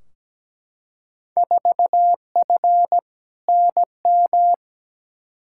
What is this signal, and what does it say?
Morse code, heard as a keyed tone, about 17 wpm, 700 Hz. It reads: 4F NM
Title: Field Day Exchanges Encore - 17wpm